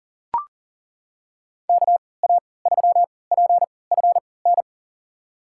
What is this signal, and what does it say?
Morse code, heard as a keyed tone, about 40 wpm, spaced as for 23 wpm, 700 Hz. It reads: KA3PFN
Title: USA Call Signs - 6 Characters - 3x Character Spacing - 40wpm